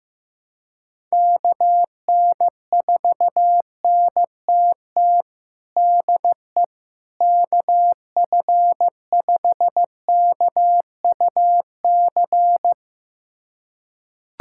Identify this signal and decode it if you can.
Morse code, heard as a keyed tone, about 16 wpm, 700 Hz. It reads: KN4NTT DE KF5KUC
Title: USA Callsign DE Callsign 15wpm